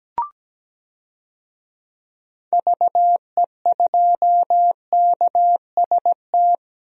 Morse code, heard as a keyed tone, about 17 wpm, 700 Hz. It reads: VE2KST
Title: DXCC-100 Call Signs - 6 Characters - 17wpm